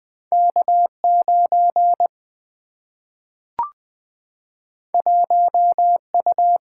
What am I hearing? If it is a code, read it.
Morse code, heard as a keyed tone, about 20 wpm, 700 Hz. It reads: K 9 1 U
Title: Single-Letter Single-Number 20wpm